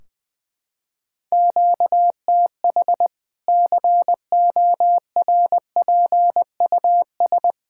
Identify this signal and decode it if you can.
Morse code, heard as a keyed tone, about 21 wpm, 700 Hz. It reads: QTH CORPUS
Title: K1USN SST QSO Practice 20wpm